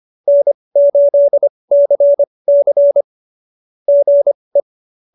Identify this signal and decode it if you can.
Morse code, heard as a keyed tone, about 25 wpm, spaced as for 20 wpm, 590 Hz. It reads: N8CC GE
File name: QSO321 20